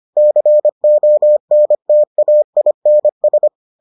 Morse code, heard as a keyed tone, about 25 wpm, 600 Hz. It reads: CONTAINS